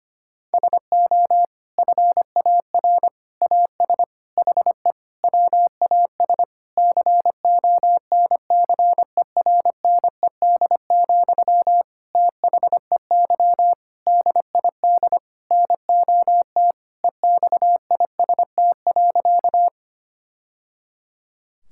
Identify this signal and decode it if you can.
Morse code, heard as a keyed tone, about 25 wpm, 700 Hz. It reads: SO FAR AS HE WAS CONCERNED, THEY DID NOT EXIST.
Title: Sentences from Top 1000 Words - Review Part 6 - 25wpm